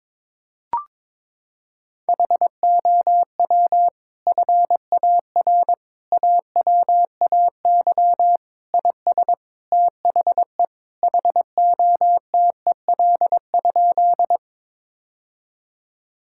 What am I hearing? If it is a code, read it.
Morse code, heard as a keyed tone, about 22 wpm, 700 Hz. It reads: HOW FAR AWAY IS THE HOTEL?
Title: Sentences from Top 1600 Words 22wpm